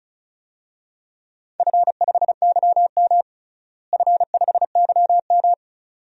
Morse code, heard as a keyed tone, about 36 wpm, 700 Hz. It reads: F5YM F5YM
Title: DXCC-100 Call Signs - 4 Characters - Repeated 2x - 35wpm